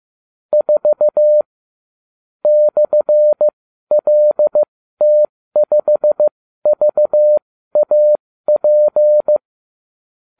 Morse code, heard as a keyed tone, about 15 wpm, spaced as for 13 wpm, 600 Hz. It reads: 4 /LT5VAP
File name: ubacw36